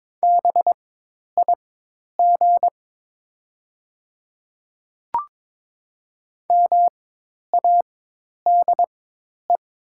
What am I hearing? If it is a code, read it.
Morse code, heard as a keyed tone, about 22 wpm, spaced as for 10 wpm, 700 Hz. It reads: BIG MADE
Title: 4x Character Spacing - Top 300 Words - 22wpm